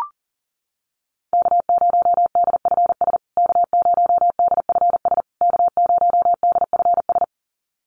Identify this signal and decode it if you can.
Morse code, heard as a keyed tone, about 40 wpm, 700 Hz. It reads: K0DFS K0DFS K0DFS
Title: USA Call Signs Repeated 3x 40wpm